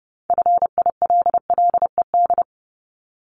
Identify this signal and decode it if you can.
Morse code, heard as a keyed tone, about 30 wpm, 700 Hz. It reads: FILLED
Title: Multi-letter Phonemes Repeated 3x 30wpm